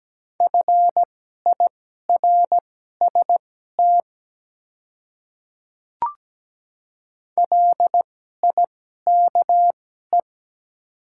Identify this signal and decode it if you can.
Morse code, heard as a keyed tone, about 17 wpm, spaced as for 12 wpm, 700 Hz. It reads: FIRST LIKE
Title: 2x Character Spacing - Top 100 Words - 17wpm